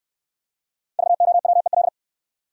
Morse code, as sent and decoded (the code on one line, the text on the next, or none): ...- --- .-- . .-..
VOWEL